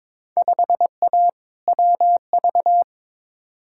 Morse code, heard as a keyed tone, about 22 wpm, 700 Hz. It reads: 5A WV
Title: Field Day Exchanges - 22wpm